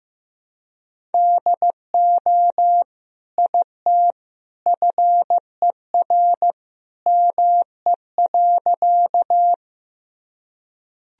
Morse code, heard as a keyed tone, about 16 wpm, 700 Hz. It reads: DO IT FER ME.
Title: Sentences from Top 200 Words and Abbreviations 15wpm